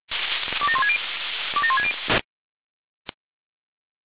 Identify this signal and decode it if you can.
Five-tone selective call (ZVEI), heard as quick runs of five tones, tones of about 80 ms. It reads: 27160 27160